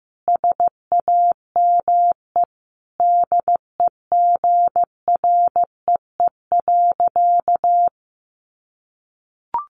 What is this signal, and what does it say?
Morse code, heard as a keyed tone, about 14 wpm, 700 Hz. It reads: SAME DEGREE.
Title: Sentences from Top 1500 Words 15wpm